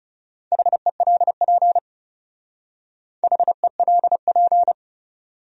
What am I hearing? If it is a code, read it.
Morse code, heard as a keyed tone, about 37 wpm, 700 Hz. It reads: HELP HELP
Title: Speed-Racing Top 300 Words 30wpm